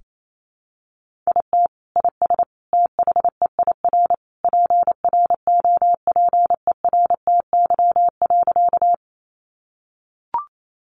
Morse code, heard as a keyed tone, about 28 wpm, 700 Hz. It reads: IT IS THEIR PROPERTY.
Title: Sentences from Top 1000 Words 28wpm